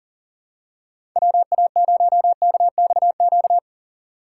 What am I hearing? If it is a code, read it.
Morse code, heard as a keyed tone, about 40 wpm, 700 Hz. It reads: WA0KXQ
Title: USA Call Signs - 6 Characters - 6x Character Spacing - 40wpm